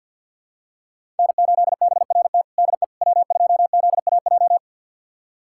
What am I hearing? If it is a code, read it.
Morse code, heard as a keyed tone, about 46 wpm, 700 Hz. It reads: N8BRT DE W2ZAJ
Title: USA Callsign DE Callsign - 5 Characters - 50wpm